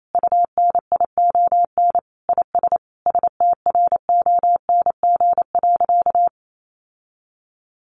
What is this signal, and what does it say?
Morse code, heard as a keyed tone, about 28 wpm, 700 Hz. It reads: UNION IS STRONG.
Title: Sentences from Top 1400 Words 28wpm